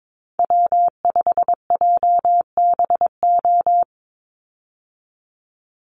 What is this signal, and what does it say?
Morse code, heard as a keyed tone, about 22 wpm, 700 Hz. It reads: W5JBO
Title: Call Signs Repeated 2x - 22wpm